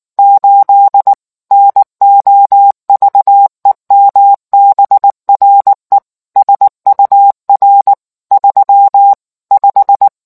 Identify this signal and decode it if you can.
Morse code, heard as a keyed tone, about 19 wpm, 800 Hz. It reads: 8 NOVEMBRE SUR 35